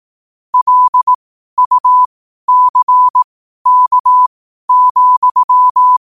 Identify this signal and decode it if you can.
Morse code, heard as a keyed tone, about 18 wpm, spaced as for 12 wpm, 1000 Hz. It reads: LUCK,
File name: qotdfiles / 12wpm